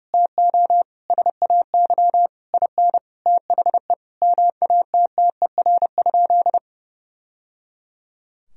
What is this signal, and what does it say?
Morse code, heard as a keyed tone, about 32 wpm, 700 Hz. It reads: TO SAY IN THE MATTER?
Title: Sentences from Top 1000 Words - Review Part 3 - 30wpm